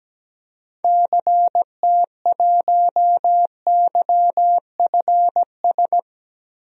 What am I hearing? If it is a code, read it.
Morse code, heard as a keyed tone, about 17 wpm, 700 Hz. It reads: CT1YFS
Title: DXCC-100 Call Signs - 6 Characters - Repeated 3x - 17wpm